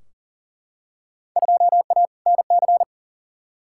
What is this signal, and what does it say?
Morse code, heard as a keyed tone, about 40 wpm, 700 Hz. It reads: 2A NC
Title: Field Day Exchanges - 40wpm